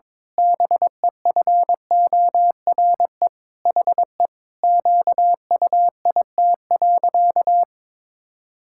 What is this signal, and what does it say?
Morse code, heard as a keyed tone, about 22 wpm, 700 Hz. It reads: BEFORE HE QUIT.
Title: Pangram Challenge v2 - 22wpm